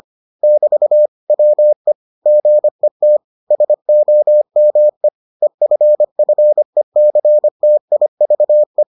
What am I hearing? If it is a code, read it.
Morse code, heard as a keyed tone, about 25 wpm, 600 Hz. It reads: <BT> WE GET SOME EFFECTIVE